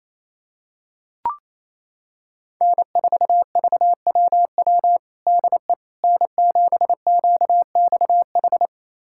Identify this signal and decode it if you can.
Morse code, heard as a keyed tone, about 28 wpm, 700 Hz. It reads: N4VWW DE N7QXH
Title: USA Callsign DE Callsign - 5 Characters - 28wpm